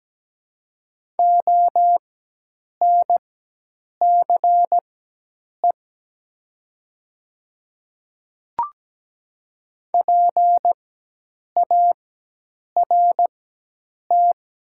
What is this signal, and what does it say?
Morse code, heard as a keyed tone, about 17 wpm, spaced as for 8 wpm, 700 Hz. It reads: ONCE PART
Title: 4x Character Spacing - Top 300 Words - 17wpm